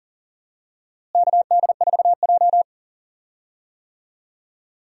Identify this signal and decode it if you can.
Morse code, heard as a keyed tone, about 40 wpm, 700 Hz. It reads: KD4J
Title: USA Call Signs - 4 Characters - Repeated 2x - 40wpm